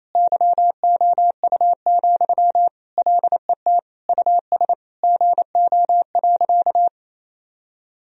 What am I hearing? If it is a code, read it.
Morse code, heard as a keyed tone, about 28 wpm, 700 Hz. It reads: YOU, LET US GO.
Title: Sentences from Top 1000 Words - Review Part 2 - 28wpm